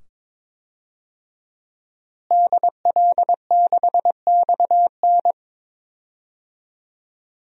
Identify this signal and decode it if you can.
Morse code, heard as a keyed tone, about 21 wpm, 700 Hz. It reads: DL6XN
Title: DXCC-100 Call Signs - 5 Characters - Repeated 2x - 22wpm